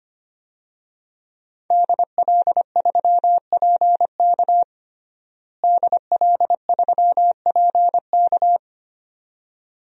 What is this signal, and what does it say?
Morse code, heard as a keyed tone, about 25 wpm, 700 Hz. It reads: DL3PK DL3PK
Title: DXCC-100 Call Signs - 5 Characters - Repeated 2x - 25wpm